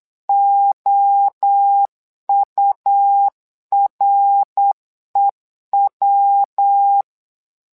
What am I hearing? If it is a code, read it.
Morse code, heard as a keyed tone, about 8 wpm, 800 Hz. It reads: OUREW
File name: LEZIONI / 3 MWRO / 3 MWRO 8wpm 2606